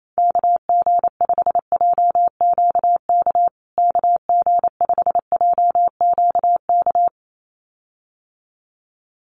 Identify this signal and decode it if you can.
Morse code, heard as a keyed tone, about 28 wpm, 700 Hz. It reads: KG5JQK KG5JQK
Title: USA Call Signs Repeated 2x 28wpm